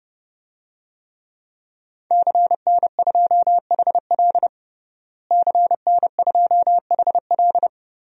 Morse code, heard as a keyed tone, about 30 wpm, 700 Hz. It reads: CN2HL CN2HL
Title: DXCC-100 Call Signs - 5 Characters - Repeated 3x - 30wpm